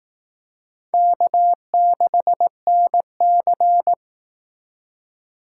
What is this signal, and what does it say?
Morse code, heard as a keyed tone, about 19 wpm, 700 Hz. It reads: K6NC